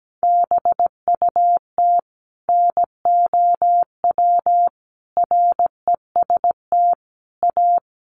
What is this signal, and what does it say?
Morse code, heard as a keyed tone, about 17 wpm, 700 Hz. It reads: BUT NOW REST A
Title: Sentences from Top 1100 Words 17wpm